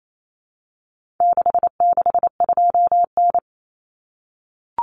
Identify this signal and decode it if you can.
Morse code, heard as a keyed tone, about 28 wpm, 700 Hz. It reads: 662N
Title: Common Cut Numbers - 4-Digit 28wpm